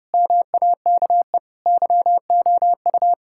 Morse code, heard as a keyed tone, about 32 wpm, 700 Hz. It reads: MAKE YOU
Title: Sentences from Top 1000 Words - Review Part 6 - 30wpm